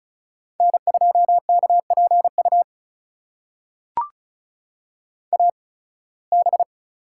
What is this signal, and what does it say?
Morse code, heard as a keyed tone, about 36 wpm, spaced as for 23 wpm, 700 Hz. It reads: N2KPU AB
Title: USA Call Signs - 5 Characters - 8x Character Spacing - 35wpm